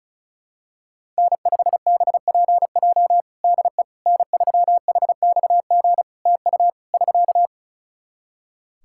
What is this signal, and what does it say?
Morse code, heard as a keyed tone, about 34 wpm, 700 Hz. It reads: N5BPJ DE N3HXG TU <SK>
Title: K1USN SST QSO Practice 35wpm